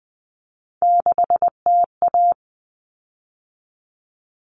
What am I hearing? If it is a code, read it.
Morse code, heard as a keyed tone, about 20 wpm, 700 Hz. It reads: 6TA
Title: Common Cut Numbers - 3-Digit 20wpm